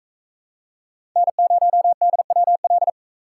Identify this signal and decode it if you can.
Morse code, heard as a keyed tone, about 44 wpm, 700 Hz. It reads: N0DWL